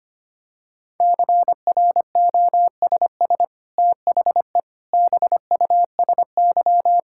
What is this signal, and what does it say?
Morse code, heard as a keyed tone, about 25 wpm, 700 Hz. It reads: CROSS THE BUSY
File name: Easy Sentences with Hints - 25wpm